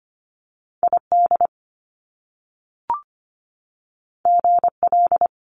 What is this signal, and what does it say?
Morse code, heard as a keyed tone, about 25 wpm, 700 Hz. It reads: ID GL
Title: Top Two Letter Combinations in Words 25wpm